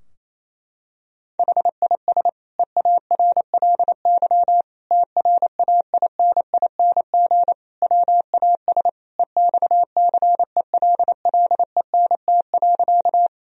Morse code, heard as a keyed tone, about 27 wpm, 700 Hz. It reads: HIS EARLY TRAINING WAS EXCELLENT.
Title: Sentences from Top 1000 Words - Review Part 4 - 28wpm